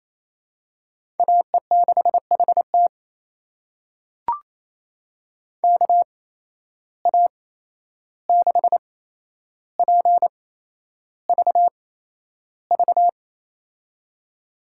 Morse code, heard as a keyed tone, about 28 wpm, spaced as for 11 wpm, 700 Hz. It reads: AE6HT KA6PVV
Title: USA Call Signs - 8x Character Spacing - 28wpm